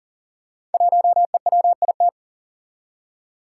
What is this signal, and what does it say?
Morse code, heard as a keyed tone, about 39 wpm, 700 Hz. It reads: 1EWIT
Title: Five Challenging Letter-Number Combinations 40wpm